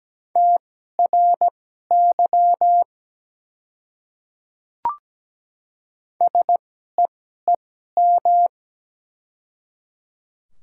Morse code, heard as a keyed tone, about 17 wpm, spaced as for 12 wpm, 700 Hz. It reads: TRY SEEM
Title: 2x Character Spacing - Top 300 Words - 17wpm